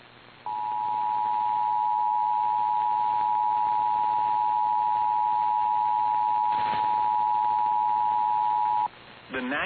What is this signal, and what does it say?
Emergency Alert System (SAME) tones (the 853 and 960 Hz attention tone), with speech. It is heard from 0.4 to 8.9 s.
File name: EAS Logs / KMNR